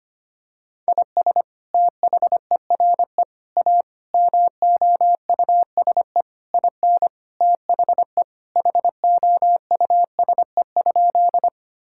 Morse code, heard as a keyed tone, about 25 wpm, 700 Hz. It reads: IS THERE A MOUSE IN THE HOUSE?
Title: Sentences from Top 1000 Words - Review Part 5 - 25wpm